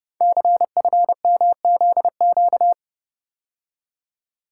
Morse code, heard as a keyed tone, about 30 wpm, 700 Hz. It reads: CFMZQ
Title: Code Groups - 5 Letters - 30wpm